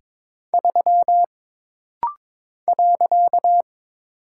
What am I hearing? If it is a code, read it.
Morse code, heard as a keyed tone, about 22 wpm, 700 Hz. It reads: 3 .
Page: This rapid-fire timing is geared to push your ICR (Instant Character Recognition) skills.